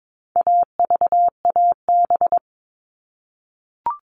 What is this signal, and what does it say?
Morse code, heard as a keyed tone, about 22 wpm, 700 Hz. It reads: AVAB
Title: All Cut Numbers - 4-Digit 22wpm